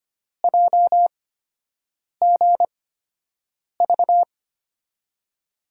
Morse code, heard as a keyed tone, about 25 wpm, spaced as for 7 wpm, 700 Hz. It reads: JGV
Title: USA Call Signs - 8x Character Spacing - 25wpm